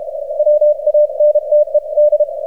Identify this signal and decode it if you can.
Morse code, heard as a keyed tone, about 32 wpm, 600 Hz. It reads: WANTED
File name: archived / CRSnet